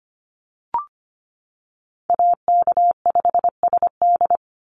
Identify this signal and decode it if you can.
Morse code, heard as a keyed tone, about 25 wpm, 700 Hz. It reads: AK5SD
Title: NA SOTA Chaser Callsigns 25wpm